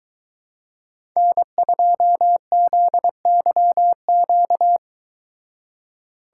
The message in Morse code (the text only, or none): N2ZYQ